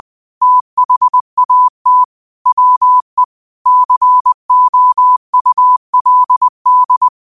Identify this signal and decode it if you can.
Morse code, heard as a keyed tone, about 20 wpm, 1000 Hz. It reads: THAT WE COULD
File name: qotdfiles / 20wpm